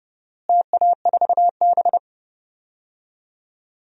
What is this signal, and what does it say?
Morse code, heard as a keyed tone, about 30 wpm, 700 Hz. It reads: TA4B
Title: DXCC-100 Call Signs - 4 Characters - Repeated 2x - 30wpm